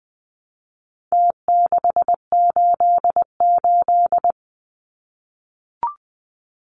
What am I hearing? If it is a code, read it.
Morse code, heard as a keyed tone, about 20 wpm, 700 Hz. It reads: T688
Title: Numbers - Contest Style - 4-Digit 20wpm